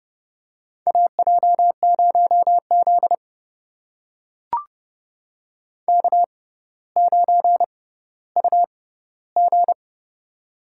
Morse code, heard as a keyed tone, about 30 wpm, spaced as for 15 wpm, 700 Hz. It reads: AJ0Z K9UG
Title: USA Call Signs - 4 Characters - 6x Character Spacing - 30wpm